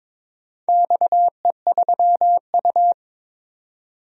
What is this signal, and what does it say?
Morse code, heard as a keyed tone, about 22 wpm, 690 Hz. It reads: XE3U
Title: DXCC-100 Call Signs - 4 Characters - 22wpm